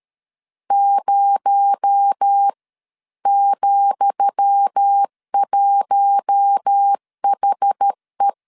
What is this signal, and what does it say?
Morse code, heard as a keyed tone, about 13 wpm, spaced as for 11 wpm, 800 Hz. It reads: 0,1HE
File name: LEZIONI / 10 SIGNS / 10D COMA 12wpm 2544